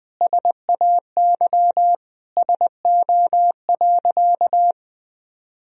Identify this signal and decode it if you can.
Morse code, heard as a keyed tone, about 20 wpm, 700 Hz. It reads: SAY SO.
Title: Sentences from Top 1000 Words - Review Part 1 - 20wpm